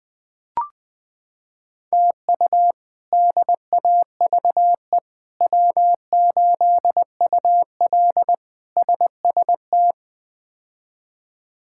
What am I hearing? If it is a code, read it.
Morse code, heard as a keyed tone, about 19 wpm, 700 Hz. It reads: TU DAVE W8UL SST
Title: K1USN SST QSO Practice - 20wpm